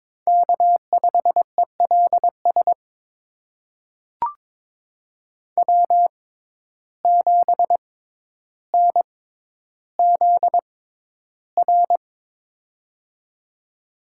Morse code, heard as a keyed tone, about 22 wpm, spaced as for 11 wpm, 700 Hz. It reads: K5ELS W7NZR
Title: USA Call Signs - 5 Characters - 6x Character Spacing - 22wpm